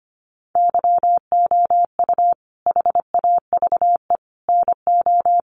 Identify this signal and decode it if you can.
Morse code, heard as a keyed tone, about 25 wpm, 700 Hz. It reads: YOU HAVE NO
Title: Sentences from Top 1500 Words 25wpm